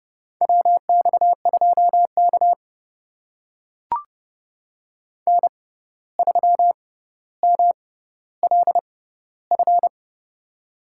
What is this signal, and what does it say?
Morse code, heard as a keyed tone, about 30 wpm, spaced as for 14 wpm, 700 Hz. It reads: WX2K N3MLF
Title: USA Call Signs - 6x Character Spacing - 30wpm